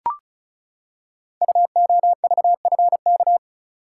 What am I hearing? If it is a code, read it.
Morse code, heard as a keyed tone, about 36 wpm, 690 Hz. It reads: UOVFK
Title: Code Groups - 5 Letters - 35wpm